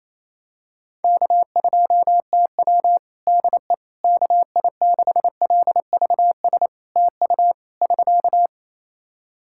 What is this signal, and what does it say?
Morse code, heard as a keyed tone, about 28 wpm, 700 Hz. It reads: K2TW DE KI6LVS TU <SK>
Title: Practice QSO - Exchange 4 - Ending - 28wpm